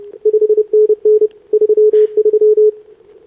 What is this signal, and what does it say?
Morse code, heard as a keyed tone, about 29 wpm, 440 Hz. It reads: 5NN 33
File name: CQWWCW2025 / SobRanCN3A 10m